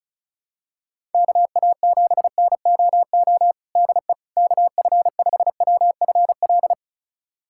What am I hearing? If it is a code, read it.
Morse code, heard as a keyed tone, about 34 wpm, 700 Hz. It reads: KA7NOO DE KF5WFL
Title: USA Callsign DE Callsign 35wpm